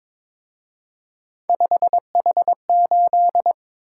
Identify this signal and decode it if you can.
Morse code, heard as a keyed tone, about 26 wpm, 700 Hz. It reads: 5H8
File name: Speed-Racing - Three Letter-Number Combinations - 22wpm